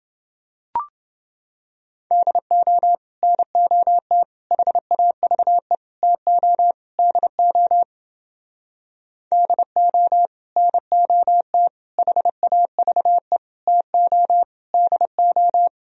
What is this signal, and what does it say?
Morse code, heard as a keyed tone, about 31 wpm, 700 Hz. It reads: DO NOT HAVE TO DO DO NOT HAVE TO DO